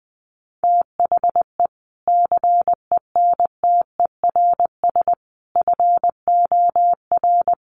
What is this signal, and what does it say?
Morse code, heard as a keyed tone, about 20 wpm, 700 Hz. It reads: THE CENTERS FOR
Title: Sets of 5 Words Encore2 20wpm